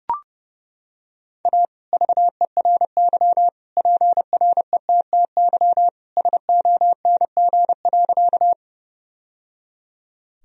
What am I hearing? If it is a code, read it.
Morse code, heard as a keyed tone, about 32 wpm, 700 Hz. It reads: A VERY PRETTY SONG.
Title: Sentences from Top 1000 Words - Review Part 2 - 30wpm